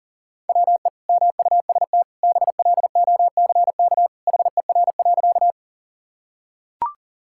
Morse code, heard as a keyed tone, about 37 wpm, 700 Hz. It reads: WE MUST BLOCK HER.
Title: Sentences from Top 1600 Words 40wpm